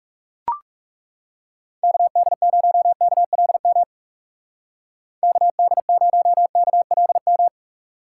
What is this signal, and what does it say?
Morse code, heard as a keyed tone, about 36 wpm, 700 Hz. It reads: KD0KLM KD0KLM
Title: Speed Racing - USA 2x3 Call Signs 30wpm